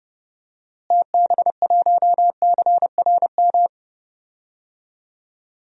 Morse code, heard as a keyed tone, about 30 wpm, 700 Hz. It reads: TB1CRM
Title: DXCC-100 Call Signs - 6 Characters - Repeated 3x - 30wpm